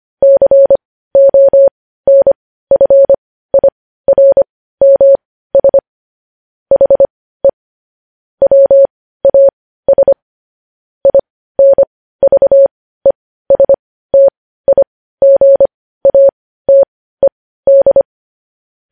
Morse code, heard as a keyed tone, about 25 wpm, spaced as for 15 wpm, 550 Hz. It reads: CONFIRMS HE WAS INVESTIGATED
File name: podcast / News headlines-1x-25x15